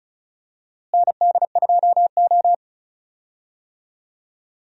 Morse code, heard as a keyed tone, about 34 wpm, 700 Hz. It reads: ND2O
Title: USA Call Signs - 4 Characters - Repeated 2x - 35wpm